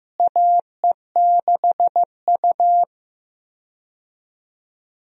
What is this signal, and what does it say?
Morse code, heard as a keyed tone, about 16 wpm, 700 Hz. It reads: AE6U
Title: USA Call Signs - 4 Characters - Repeated 2x - 15wpm